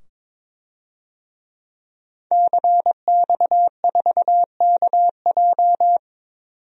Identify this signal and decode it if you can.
Morse code, heard as a keyed tone, about 22 wpm, 700 Hz. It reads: CX4KJ
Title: DXCC-100 Call Signs - 5 Characters - 22wpm